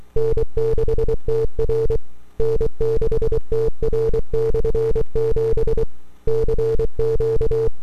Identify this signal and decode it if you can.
Morse code, heard as a keyed tone, about 24 wpm, 450 Hz. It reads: N6TR N6TR/7 CQ
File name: n6tr / 12Oct2009-160